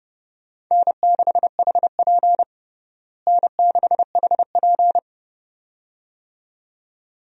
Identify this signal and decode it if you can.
Morse code, heard as a keyed tone, about 32 wpm, 700 Hz. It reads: N6HP N6HP
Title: USA Call Signs - 4 Characters - Repeated 2x - 30wpm